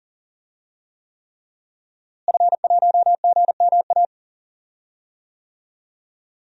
Morse code, heard as a keyed tone, about 37 wpm, 700 Hz. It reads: F1GMA
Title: DXCC-100 Call Signs - 5 Characters - Repeated 3x - 40wpm